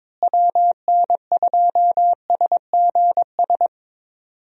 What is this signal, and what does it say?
Morse code, heard as a keyed tone, about 22 wpm, 700 Hz. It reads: WN2SGS
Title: USA Call Signs - 6 Characters - Repeated 3x - 22wpm